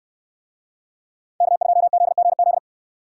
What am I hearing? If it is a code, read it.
Morse code, heard as a keyed tone, about 65 wpm, 700 Hz. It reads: K2ZGZ